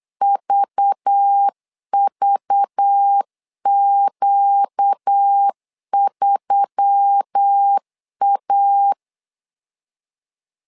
Morse code, heard as a keyed tone, about 8 wpm, 800 Hz. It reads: VVQ3A
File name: LEZIONI / 5 LBVC / 5 LBV 8wpm 2837